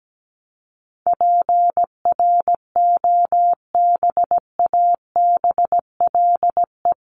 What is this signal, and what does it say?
Morse code, heard as a keyed tone, about 17 wpm, 700 Hz. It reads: PROBABLE
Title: LY Suffix - Top 100 Words with Base Words 17wpm